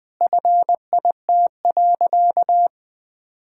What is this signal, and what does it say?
Morse code, heard as a keyed tone, about 20 wpm, 700 Hz. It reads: FIT.
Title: Sentences with 2-Word Phrases - 20wpm